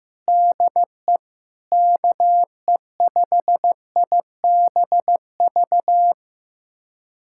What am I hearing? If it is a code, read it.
Morse code, heard as a keyed tone, about 15 wpm, 700 Hz. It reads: DE KE5IBV
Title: USA Callsign DE Callsign 15wpm